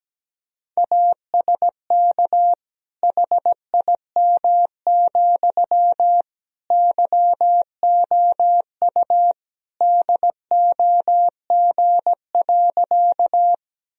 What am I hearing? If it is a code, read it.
Morse code, heard as a keyed tone, about 17 wpm, 700 Hz. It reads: ASK HIM, YOU DOG.
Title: Sentences from Top 1000 Words - Review Part 2 - 17wpm